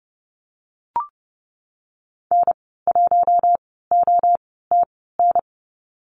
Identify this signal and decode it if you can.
Morse code, heard as a keyed tone, about 30 wpm, spaced as for 17 wpm, 700 Hz. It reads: N1OTN
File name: Lesson 14g - Number 2 - Call Signs 30wpm